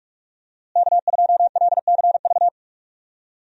Morse code, heard as a keyed tone, about 46 wpm, 700 Hz. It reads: K2LCV
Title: USA Call Signs - 5 Characters - Repeated 2x - 45wpm